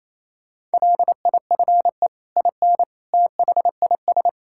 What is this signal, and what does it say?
Morse code, heard as a keyed tone, about 28 wpm, 700 Hz. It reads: LIFE IN THIS
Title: Sentences from Top 1000 Words - Review Part 5 - 28wpm